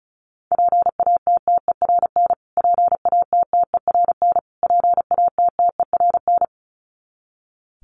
Morse code, heard as a keyed tone, about 34 wpm, 700 Hz. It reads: PATTERN PATTERN PATTERN
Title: Top 500 Words Repeated 35wpm